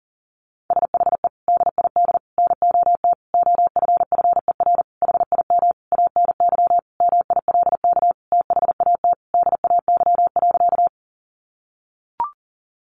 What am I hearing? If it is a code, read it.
Morse code, heard as a keyed tone, about 40 wpm, 700 Hz. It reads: SHE DID NOT OFFER HIM ANY MILK THAT DAY.
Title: Sentences from Top 1500 Words 40wpm